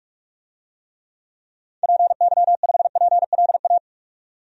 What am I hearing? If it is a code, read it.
Morse code, heard as a keyed tone, about 43 wpm, 700 Hz. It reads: PY5PLA
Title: DXCC-100 Call Signs - 6 Characters - 45wpm